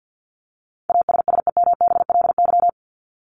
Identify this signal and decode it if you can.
Morse code, heard as a keyed tone, about 50 wpm, 700 Hz. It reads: ASSEMBLY
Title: LY Suffix - Top 100 Words with Base Words 50wpm